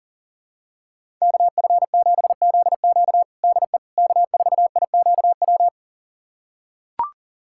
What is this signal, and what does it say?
Morse code, heard as a keyed tone, about 40 wpm, 700 Hz. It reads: KF7ZQ DE K4IQW
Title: USA Callsign DE Callsign - 5 Characters - 40wpm